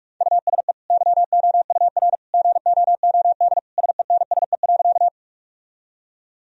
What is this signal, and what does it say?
Morse code, heard as a keyed tone, about 46 wpm, 700 Hz. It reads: USE YOUR GOOD SENSE.
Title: Sentences from Top 1000 Words - Review Part 4 - 45wpm